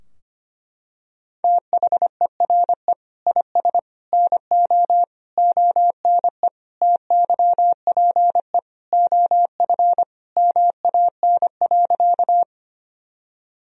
Morse code, heard as a keyed tone, about 25 wpm, 700 Hz. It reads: THERE IS NO ONE TYPE OF MAN.
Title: Sentences from Top 1000 Words - Review Part 3 - 25wpm